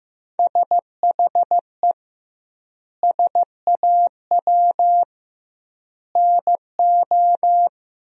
Morse code, heard as a keyed tone, about 15 wpm, 700 Hz. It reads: SHE SAW NO
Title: 2x Word-Spacing - Sentences from Top 300 Words - 15wpm